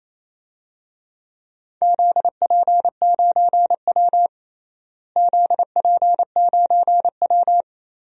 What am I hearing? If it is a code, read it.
Morse code, heard as a keyed tone, about 28 wpm, 700 Hz. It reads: ZP9W ZP9W
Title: DXCC-100 Call Signs - 4 Characters - Repeated 2x - 28wpm